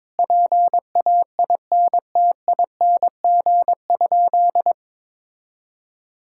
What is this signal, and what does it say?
Morse code, heard as a keyed tone, about 22 wpm, 700 Hz. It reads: PAINTING?
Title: Sentences from Top 1000 Words - Review Part 6 - 22wpm